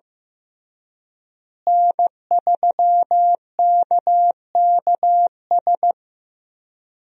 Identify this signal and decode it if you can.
Morse code, heard as a keyed tone, about 15 wpm, 700 Hz. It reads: N3KKS
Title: USA Call Signs - 5 Characters - Repeated 3x - 15wpm